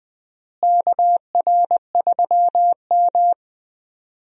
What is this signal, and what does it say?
Morse code, heard as a keyed tone, about 20 wpm, 700 Hz. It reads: KR3M
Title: USA Call Signs - 4 Characters - Repeated 3x - 20wpm